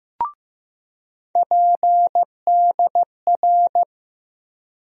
Morse code, heard as a keyed tone, about 15 wpm, 700 Hz. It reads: PDR
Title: USA Call Sign Postfixes - 3 Letters 15wpm